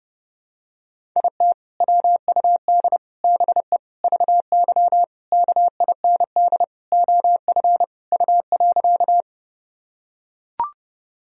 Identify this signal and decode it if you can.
Morse code, heard as a keyed tone, about 32 wpm, 700 Hz. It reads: IT WUD BE VY KIND OF U.
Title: Sentences from Top 200 Words and Abbreviations 30wpm